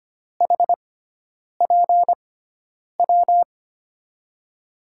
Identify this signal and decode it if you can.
Morse code, heard as a keyed tone, about 25 wpm, spaced as for 9 wpm, 700 Hz. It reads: HPW
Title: USA Call Signs - 6 Characters - 6x Character Spacing - 25wpm